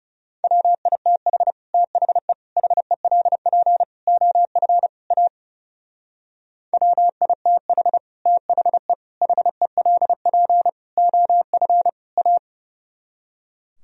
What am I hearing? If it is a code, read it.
Morse code, heard as a keyed tone, about 39 wpm, 700 Hz. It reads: WITH THE HELP OF A WITH THE HELP OF A